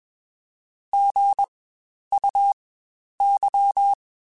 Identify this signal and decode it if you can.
Morse code, heard as a keyed tone, about 21 wpm, spaced as for 10 wpm, 800 Hz. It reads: GUY
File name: Spaced / Chuckles Spaced-mp3s / HumanErr sp @20